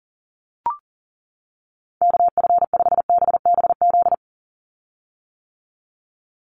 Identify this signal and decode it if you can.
Morse code, heard as a keyed tone, about 40 wpm, 700 Hz. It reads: KF5BBZ
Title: USA Call Signs 40wpm